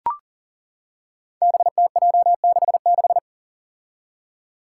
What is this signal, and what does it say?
Morse code, heard as a keyed tone, about 42 wpm, 700 Hz. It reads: BTJ66
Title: Five Challenging Letter-Number Combinations 40wpm